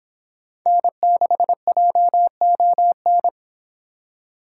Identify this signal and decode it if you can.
Morse code, heard as a keyed tone, about 25 wpm, 700 Hz. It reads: N6JON